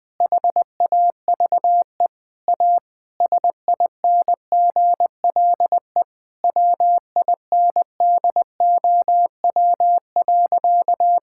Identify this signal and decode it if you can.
Morse code, heard as a keyed tone, about 20 wpm, 700 Hz. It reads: HAVE A SINGLE WINDOW.
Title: Sentences from Top 1600 Words 20wpm